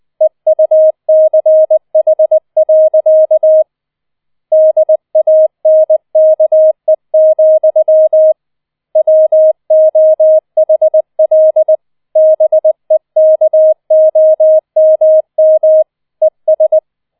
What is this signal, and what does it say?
Morse code, heard as a keyed tone, about 20 wpm, 630 Hz. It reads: EUCH. DANKE, WOHL BEKOMM ES
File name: 20wpm